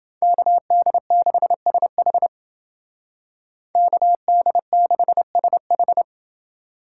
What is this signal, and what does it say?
Morse code, heard as a keyed tone, about 29 wpm, 700 Hz. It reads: KD6SH KD6SH